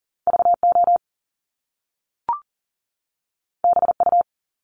Morse code, heard as a keyed tone, about 39 wpm, 700 Hz. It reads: VO BU
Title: Top Two Letter Combinations in Words 40wpm